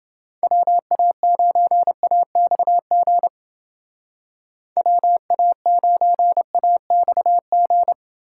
Morse code, heard as a keyed tone, about 29 wpm, 700 Hz. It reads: WA9AXG WA9AXG